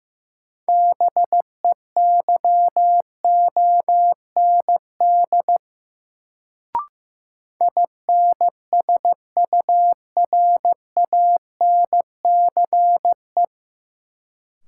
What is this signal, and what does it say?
Morse code, heard as a keyed tone, about 16 wpm, 700 Hz. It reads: BEYOND INSURANCE
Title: Sentences from Top 1100 Words - New Word Drill Alt - 15wpm